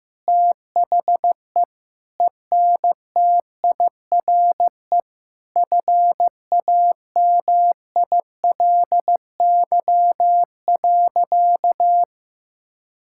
Morse code, heard as a keyed tone, about 15 wpm, 700 Hz. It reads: THE ENTIRE FAMILY.
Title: Sentences from Top 1000 Words - Review Part 5 - 15wpm